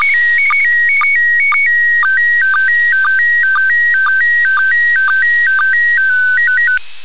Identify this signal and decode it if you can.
SSTV (slow-scan television), PD 120.
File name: commit / satnogs 29407 short